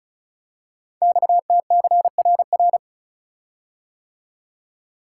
Morse code, heard as a keyed tone, about 36 wpm, 700 Hz. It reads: XTCRR
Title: Code Groups - 5 Letters - 35wpm